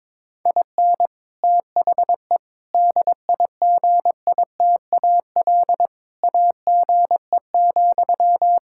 Morse code, heard as a keyed tone, about 22 wpm, 700 Hz. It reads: IN THE DIGITAL AGE,
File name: Difficult Sentences with Hints - 22wpm